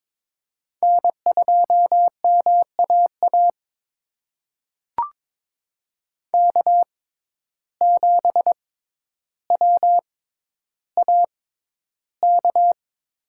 Morse code, heard as a keyed tone, about 21 wpm, spaced as for 11 wpm, 700 Hz. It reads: N2MAA K7WAK
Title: USA Call Signs - 5 Characters - 6x Character Spacing - 22wpm